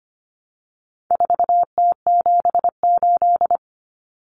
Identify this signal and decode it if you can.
Morse code, heard as a keyed tone, about 25 wpm, 700 Hz. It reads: 4T78
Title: Common Cut Numbers - 4-Digit 25wpm